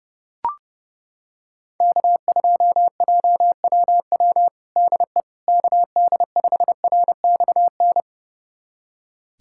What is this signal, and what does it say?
Morse code, heard as a keyed tone, about 30 wpm, 700 Hz. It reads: K2JWW DE KD5RXN
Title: USA Callsign DE Callsign 30wpm